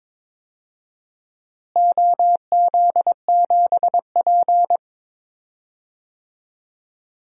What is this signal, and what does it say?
Morse code, heard as a keyed tone, about 21 wpm, 700 Hz. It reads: OZ7P
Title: DXCC-100 Call Signs - 4 Characters - Repeated 2x - 22wpm